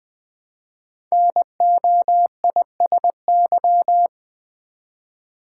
Morse code, heard as a keyed tone, about 20 wpm, 700 Hz. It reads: NOISY
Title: Build A Word - 1 Letter At A Time - 3 to 5-Letter Words - 20wpm